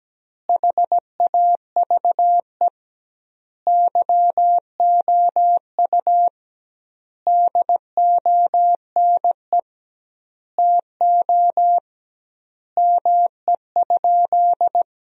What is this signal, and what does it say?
Morse code, heard as a keyed tone, about 17 wpm, 700 Hz. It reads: HAVE YOU DONE TO ME?
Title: 2x Word-Spacing - Sentences from Top 500 Words - 17wpm